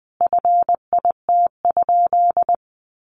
Morse code, heard as a keyed tone, about 20 wpm, 700 Hz. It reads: FIT?
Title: Sentences from Top 800 Words 20wpm